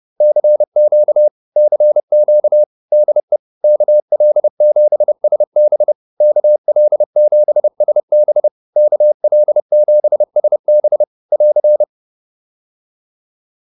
Morse code, heard as a keyed tone, about 30 wpm, 600 Hz. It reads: CQ CQ DE KL7SB KL7SB KL7SB <AR>